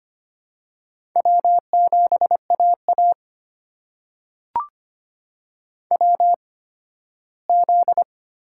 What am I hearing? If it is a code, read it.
Morse code, heard as a keyed tone, about 25 wpm, spaced as for 15 wpm, 700 Hz. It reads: W7AA WZ
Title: USA Call Signs - 4 Characters - 8x Character Spacing - 25wpm